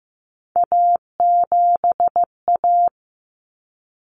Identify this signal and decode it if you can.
Morse code, heard as a keyed tone, about 15 wpm, 700 Hz. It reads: A7A
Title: Common Cut Numbers - 3-Digit 15wpm